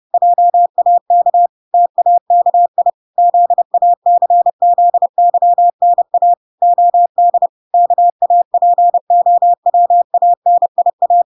Morse code, heard as a keyed tone, about 30 wpm, 700 Hz. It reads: JAK TAKI ZACZYNA OD KAPOWANIA